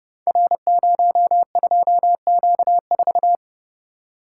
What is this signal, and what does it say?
Morse code, heard as a keyed tone, about 30 wpm, 700 Hz. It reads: R02Q4
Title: Code Groups - 5 Letter:Numbers - 30wpm